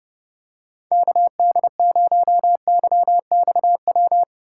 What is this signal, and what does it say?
Morse code, heard as a keyed tone, about 31 wpm, 700 Hz. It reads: KD0YXW